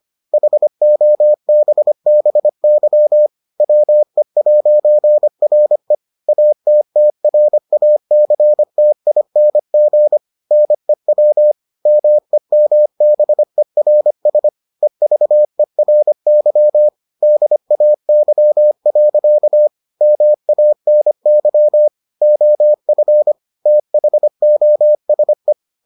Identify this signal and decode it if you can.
Morse code, heard as a keyed tone, about 25 wpm, 600 Hz. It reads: HOBBY WE'RE ATTRACTING NEW MEMBERS EVERY DAY. MANY OF THOSE